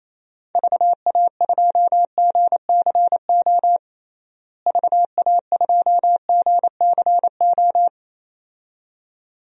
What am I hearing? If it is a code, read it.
Morse code, heard as a keyed tone, about 27 wpm, 700 Hz. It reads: VA2GCO VA2GCO
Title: DXCC-100 Call Signs - 6 Characters - Repeated 2x - 28wpm